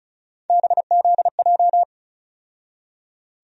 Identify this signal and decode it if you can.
Morse code, heard as a keyed tone, about 36 wpm, 700 Hz. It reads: BZJ
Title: USA Call Sign Postfixes - 3 Letters 35wpm